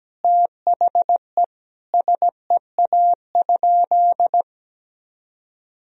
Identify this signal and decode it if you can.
Morse code, heard as a keyed tone, about 17 wpm, 700 Hz. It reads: THE SEA?
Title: Sentences from Top 1000 Words - Review Part 3 - 17wpm